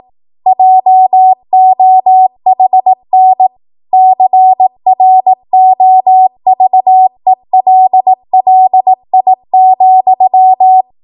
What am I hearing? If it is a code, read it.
Morse code, heard as a keyed tone, about 18 wpm, 750 Hz. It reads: JOHN CROVELLI,